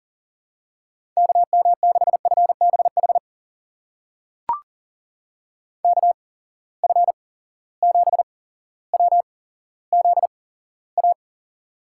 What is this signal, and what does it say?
Morse code, heard as a keyed tone, about 40 wpm, spaced as for 17 wpm, 700 Hz. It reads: KM6FBH KF7WZA
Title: USA Call Signs - 6 Characters - 8x Character Spacing - 40wpm